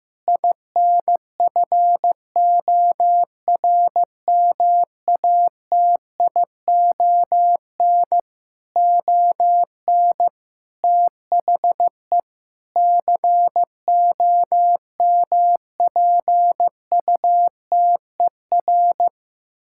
Morse code, heard as a keyed tone, about 15 wpm, 700 Hz. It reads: INFORMATION ON THE COMPUTER
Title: Sentences from Top 1000 Words - Review Part 6 - 15wpm